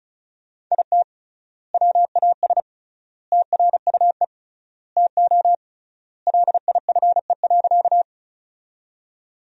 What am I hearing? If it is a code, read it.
Morse code, heard as a keyed tone, about 36 wpm, 700 Hz. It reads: IT WAS TRUE TO LIFE.
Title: 3x Word-Spacing - Sentences from Top 400 Words - 35wpm